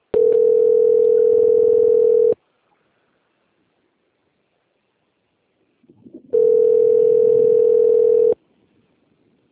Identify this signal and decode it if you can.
Ringback tone (North America), 440 + 480 Hz.